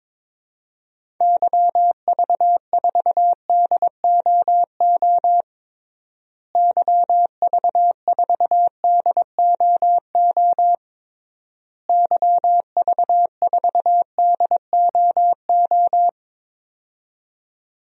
Morse code, heard as a keyed tone, about 22 wpm, 700 Hz. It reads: YV4DOO YV4DOO YV4DOO
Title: DXCC-100 Call Signs - 6 Characters - Repeated 3x - 22wpm